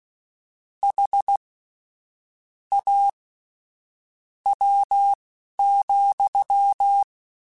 Morse code, heard as a keyed tone, about 16 wpm, spaced as for 7 wpm, 800 Hz. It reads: HAW,
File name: Misc / Misc-mp3s / WordPairs sp @10